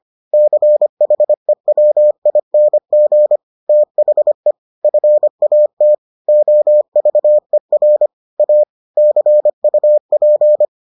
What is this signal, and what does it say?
Morse code, heard as a keyed tone, about 25 wpm, 600 Hz. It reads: CHEWING THE FAT OVER A CUP